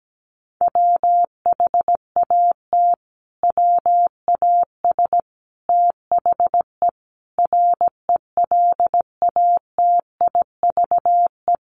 Morse code, heard as a keyed tone, about 17 wpm, 700 Hz. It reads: WHAT WAS THE RELATIVE
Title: Sentences from Top 1400 Words 17wpm